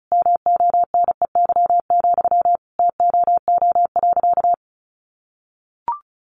Morse code, heard as a keyed tone, about 34 wpm, 700 Hz. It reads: MONEY, TOO.
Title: Sentences from Top 900 Words 35wpm